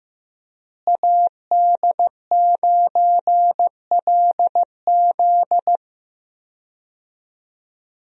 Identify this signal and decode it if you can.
Morse code, heard as a keyed tone, about 16 wpm, 700 Hz. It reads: AD9LZ
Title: USA Call Signs - 5 Characters - Repeated 3x - 15wpm